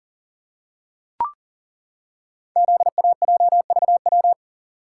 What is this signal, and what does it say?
Morse code, heard as a keyed tone, about 39 wpm, 700 Hz. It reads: ZAJVW
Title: Code Groups - 5 Letters - 40wpm